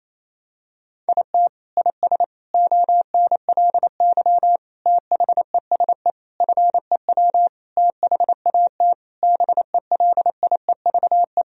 Morse code, heard as a keyed tone, about 28 wpm, 700 Hz. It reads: IT IS ONLY THESE FEW THAT BELIEVE
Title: Sentences from Top 1000 Words - Review Part 1 - 28wpm